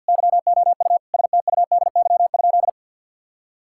Morse code, heard as a keyed tone, about 50 wpm, 700 Hz. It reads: YOU STUDY?
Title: Sentences from Top 1000 Words - Review Part 6 - 50wpm